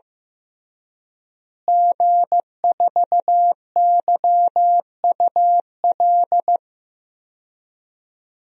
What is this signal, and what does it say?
Morse code, heard as a keyed tone, about 15 wpm, 700 Hz. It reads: G4YUL
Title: DXCC-100 Call Signs - 5 Characters - Repeated 3x - 15wpm